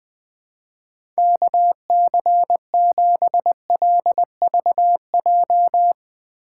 Morse code, heard as a keyed tone, about 20 wpm, 700 Hz. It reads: KC7LVJ
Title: Speed Racing - USA 2x3 Call Signs 17wpm